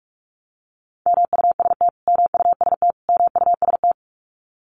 Morse code, heard as a keyed tone, about 44 wpm, 700 Hz. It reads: MUST MUST MUST
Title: Top 500 Words Repeated 45wpm